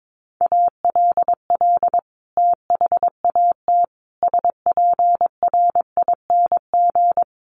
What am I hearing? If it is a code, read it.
Morse code, heard as a keyed tone, about 22 wpm, 700 Hz. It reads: ALL THAT SPRING
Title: Sentences from Top 1200 Words 22wpm